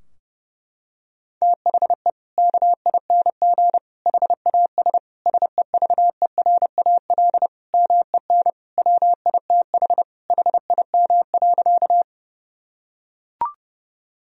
Morse code, heard as a keyed tone, about 30 wpm, 700 Hz. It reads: THE KING HAS SEVERAL MEN WITH HIM.
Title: Sentences from Top 1000 Words - Review Part 2 - 30wpm